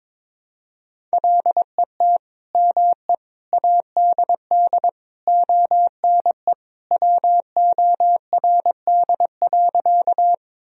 Morse code, heard as a keyed tone, about 22 wpm, 700 Hz. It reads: LET ME ADD ONE WORD.
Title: Sentences from Top 1000 Words - Review Part 1 - 22wpm